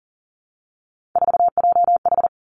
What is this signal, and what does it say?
Morse code, heard as a keyed tone, about 42 wpm, 700 Hz. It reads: 4JH
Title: Three Challenging Letter-Number Combinations 40wpm